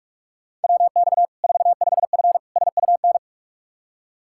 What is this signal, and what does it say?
Morse code, heard as a keyed tone, about 46 wpm, 700 Hz. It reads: WX 45F SUN
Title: K1USN SST QSO Practice 45wpm